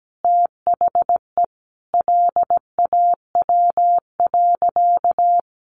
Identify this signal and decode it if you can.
Morse code, heard as a keyed tone, about 17 wpm, 700 Hz. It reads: THE LAW.
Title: Sentences from Top 1400 Words 17wpm